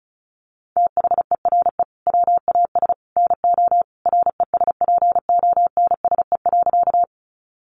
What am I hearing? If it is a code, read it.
Morse code, heard as a keyed tone, about 34 wpm, 700 Hz. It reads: THERE WAS NO RESPONSE.
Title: Sentences from Top 1400 Words 35wpm